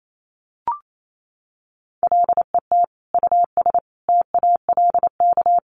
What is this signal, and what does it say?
Morse code, heard as a keyed tone, about 28 wpm, 700 Hz. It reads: LET US TALK
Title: Sentences from Top 400 Words 28wpm